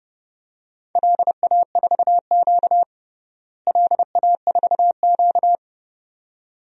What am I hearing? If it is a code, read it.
Morse code, heard as a keyed tone, about 30 wpm, 700 Hz. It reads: LA4Q LA4Q
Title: DXCC-100 Call Signs - 4 Characters - Repeated 2x - 30wpm